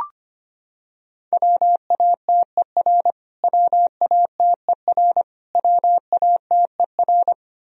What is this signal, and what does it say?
Morse code, heard as a keyed tone, about 25 wpm, 700 Hz. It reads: WATER WATER WATER
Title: Top 100 Words Repeated 25wpm